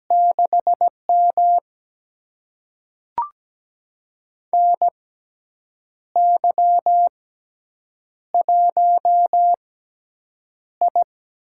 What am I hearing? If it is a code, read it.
Morse code, heard as a keyed tone, about 17 wpm, spaced as for 7 wpm, 700 Hz. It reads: 6M NY1I
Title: USA Call Signs - 4 Characters - 6x Character Spacing - 17wpm